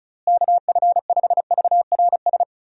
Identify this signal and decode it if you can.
Morse code, heard as a keyed tone, about 36 wpm, 700 Hz. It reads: KF5VRS
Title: USA Call Signs - 6 Characters - Repeated 2x - 35wpm